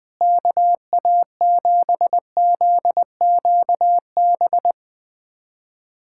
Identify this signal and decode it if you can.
Morse code, heard as a keyed tone, about 21 wpm, 700 Hz. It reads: KA7ZQB
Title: USA Callsign DE Callsign - 6 Characters - 20wpm